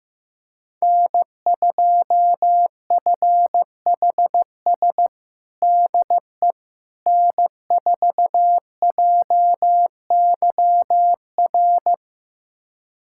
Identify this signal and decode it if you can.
Morse code, heard as a keyed tone, about 15 wpm, 700 Hz. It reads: N2FHS DE N4JYR
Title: USA Callsign DE Callsign - 5 Characters - 15wpm